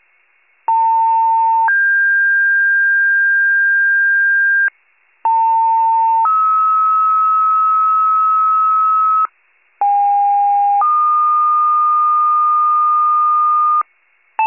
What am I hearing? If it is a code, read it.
Two-tone page, heard as a one-second tone, then a three-second tone, about 900 then 1600 Hz, about 900 then 1290 Hz, about 810 then 1220 Hz. It is heard from 0.7 to 13.8 s.
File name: wcalert